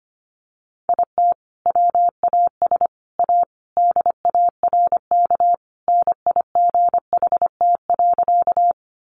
Morse code, heard as a keyed tone, about 25 wpm, 700 Hz. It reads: IT WAS A DARK NIGHT.
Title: Sentences from Top 500 Words 25wpm